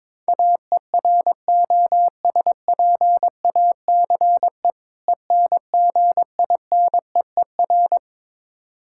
Morse code, heard as a keyed tone, about 21 wpm, 700 Hz. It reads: AEROSPACE ENGINEER
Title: K1USN SST QSO Practice 22wpm